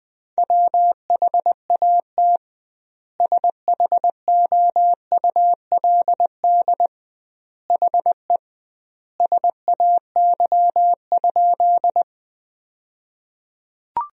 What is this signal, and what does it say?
Morse code, heard as a keyed tone, about 20 wpm, 700 Hz. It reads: WHAT SHOULD HE SAY?
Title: 2x Word-Spacing - Sentences from Top 300 Words - 20wpm